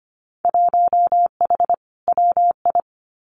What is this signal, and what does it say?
Morse code, heard as a keyed tone, about 25 wpm, 700 Hz. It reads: 1H WI
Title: Winter Field Day Exchanges 25wpm